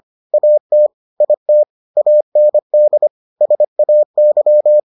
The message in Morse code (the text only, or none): AT IT AND SAY